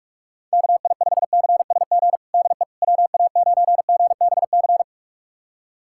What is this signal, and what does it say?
Morse code, heard as a keyed tone, about 42 wpm, 700 Hz. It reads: KI5CSG DE WA9GBC
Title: USA Callsign DE Callsign - 6 Characters - 45wpm